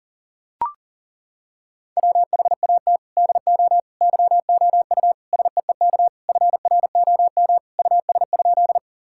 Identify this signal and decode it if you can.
Morse code, heard as a keyed tone, about 37 wpm, 700 Hz. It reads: WHAT DO YOU SEEK FROM US?
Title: Sentences from Top 1600 Words 40wpm